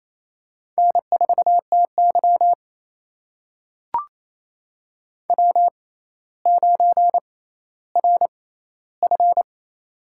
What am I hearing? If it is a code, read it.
Morse code, heard as a keyed tone, about 28 wpm, spaced as for 14 wpm, 700 Hz. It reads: N4TY W9RF
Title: USA Call Signs - 4 Characters - 6x Character Spacing - 28wpm